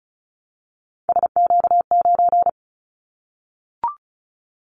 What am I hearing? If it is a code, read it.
Morse code, heard as a keyed tone, about 36 wpm, 700 Hz. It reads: SQ9
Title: Callsign Prefixes for Top 25 DXCC - 35wpm